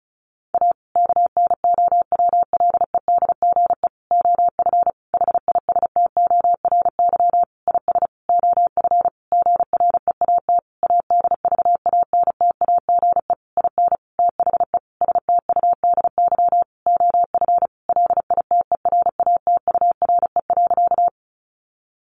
Morse code, heard as a keyed tone, about 34 wpm, 700 Hz. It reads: A KNOWLEDGE OF HISTORY IS OF GREAT ADVANTAGE IN THE STUDY OF LITERATURE.
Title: Sentences from Top 900 Words 35wpm